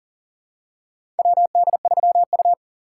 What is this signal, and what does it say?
Morse code, heard as a keyed tone, about 40 wpm, 700 Hz. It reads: WD3U
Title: USA Call Signs - 4 Characters - 2x Character Spacing - 40wpm